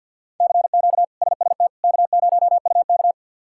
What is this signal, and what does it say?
Morse code, heard as a keyed tone, about 47 wpm, 690 Hz. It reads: CQ SST K0UK
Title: K1USN SST QSO Practice - 50wpm